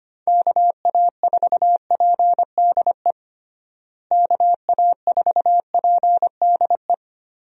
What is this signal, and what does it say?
Morse code, heard as a keyed tone, about 25 wpm, 700 Hz. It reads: KA4PDE KA4PDE
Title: USA Call Signs - 6 Characters - Repeated 2x - 25wpm